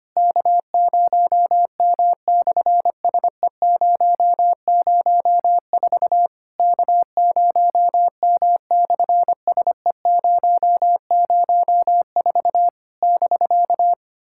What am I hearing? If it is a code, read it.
Morse code, heard as a keyed tone, about 25 wpm, 700 Hz. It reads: K0M/SE004 K0M/SE004 <BK>
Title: SOTA QSO Practice 25wpm